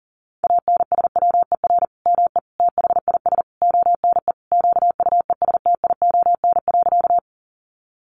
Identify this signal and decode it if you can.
Morse code, heard as a keyed tone, about 40 wpm, 700 Hz. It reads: ANSWER ME THIS ONE QUESTION.
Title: Sentences from Top 400 Words 40wpm